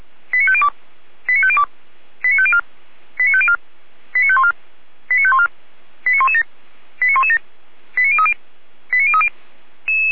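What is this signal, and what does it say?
Five-tone selective call (ZVEI), heard as quick runs of five tones, tones of about 70 ms. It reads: 89502 89502 89504 89504 89514 89514 89117 89117 89030 89030
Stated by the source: ZVEI five-tone selective call received by SDR.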